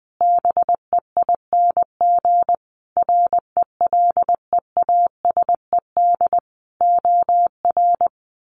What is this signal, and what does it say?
Morse code, heard as a keyed tone, about 20 wpm, 700 Hz. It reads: BEING RELEASED OR
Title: A Sound Of Thunder 20wpm